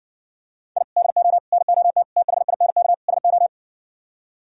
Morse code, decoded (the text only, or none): I DO NOT THINK SO